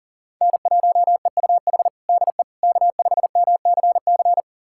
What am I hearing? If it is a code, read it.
Morse code, heard as a keyed tone, about 40 wpm, 700 Hz. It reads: N1EUH DE K5MCC
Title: USA Callsign DE Callsign - 5 Characters - 40wpm